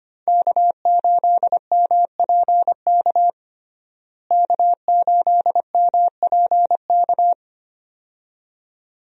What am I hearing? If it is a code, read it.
Morse code, heard as a keyed tone, about 25 wpm, 700 Hz. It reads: K8MPK K8MPK
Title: USA Call Signs - 5 Characters - Repeated 2x - 25wpm